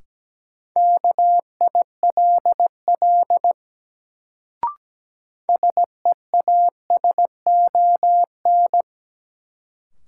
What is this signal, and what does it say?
Morse code, heard as a keyed tone, about 17 wpm, 700 Hz. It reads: KILL SEASON
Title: Sentences from Top 1400 Words - New Word Drill Alt - 17wpm